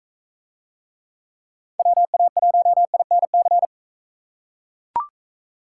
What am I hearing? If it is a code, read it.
Morse code, heard as a keyed tone, about 41 wpm, 700 Hz. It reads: WA1INC